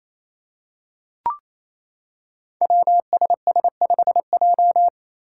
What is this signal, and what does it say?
Morse code, heard as a keyed tone, about 28 wpm, 700 Hz. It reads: WSS5J
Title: Five Challenging Letter-Number Combinations 28wpm